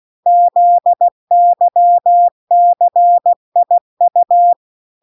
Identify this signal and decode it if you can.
Morse code, heard as a keyed tone, about 16 wpm, 700 Hz. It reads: ZYCIU